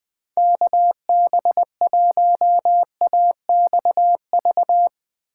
Morse code, heard as a keyed tone, about 20 wpm, 700 Hz. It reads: KB1AXV